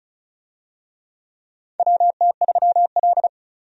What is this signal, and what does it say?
Morse code, heard as a keyed tone, about 34 wpm, 700 Hz. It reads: WT3L
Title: USA Call Signs - 4 Characters - 35wpm